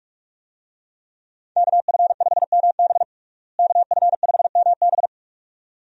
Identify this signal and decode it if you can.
Morse code, heard as a keyed tone, about 42 wpm, 700 Hz. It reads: KF5MB KF5MB
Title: USA Call Signs - 5 Characters - Repeated 2x - 45wpm